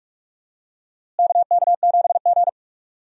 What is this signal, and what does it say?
Morse code, heard as a keyed tone, about 42 wpm, 700 Hz. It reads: KK7G
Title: USA Call Signs - 4 Characters - Repeated 3x - 45wpm